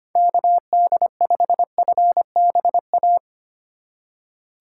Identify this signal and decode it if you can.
Morse code, heard as a keyed tone, about 25 wpm, 700 Hz. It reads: KD5FBA
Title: K1USN SST QSO Practice - 25wpm